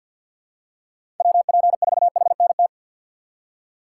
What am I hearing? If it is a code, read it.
Morse code, heard as a keyed tone, about 53 wpm, 700 Hz. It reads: WP4HNT
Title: USA Call Signs - 6 Characters - 4x Character Spacing - 50wpm